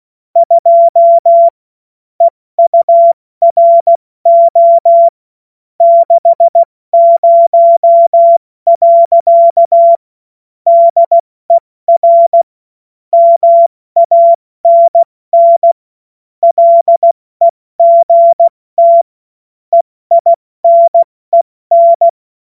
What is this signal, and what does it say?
Morse code, heard as a keyed tone, about 16 wpm, 680 Hz. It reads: 2 EURO 60. DER MANN LEGT EINEN
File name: Morsekurs / Witze / Witz 11. Mai 2021, 70 F80